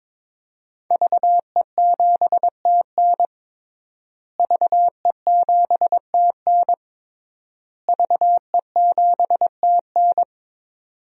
Morse code, heard as a keyed tone, about 22 wpm, 700 Hz. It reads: VE7TN VE7TN VE7TN
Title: DXCC-100 Call Signs - 5 Characters - Repeated 3x - 22wpm